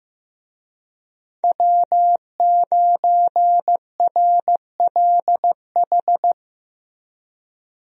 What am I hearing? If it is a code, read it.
Morse code, heard as a keyed tone, about 15 wpm, 700 Hz. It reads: W9RLH
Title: USA Call Signs - 5 Characters - Repeated 2x - 15wpm